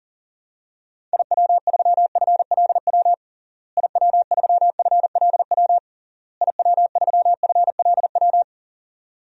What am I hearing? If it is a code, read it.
Morse code, heard as a keyed tone, about 40 wpm, 700 Hz. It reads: IW3FLW IW3FLW IW3FLW
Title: DXCC-100 Call Signs - 6 Characters - Repeated 3x - 40wpm